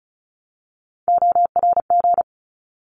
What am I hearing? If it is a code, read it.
Morse code, heard as a keyed tone, about 34 wpm, 700 Hz. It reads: ORG
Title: ARRL Sections 35wpm